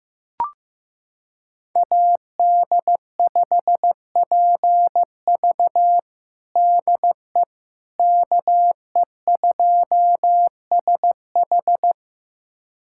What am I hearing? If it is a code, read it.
Morse code, heard as a keyed tone, about 15 wpm, 700 Hz. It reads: AD5PV DE KE2SH
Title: USA Callsign DE Callsign - 5 Characters - 15wpm